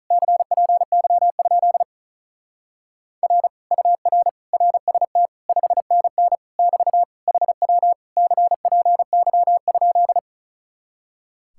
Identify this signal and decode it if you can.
Morse code, heard as a keyed tone, about 45 wpm, 700 Hz. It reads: CPY? R UR RST 5NN <BT> HW CPY?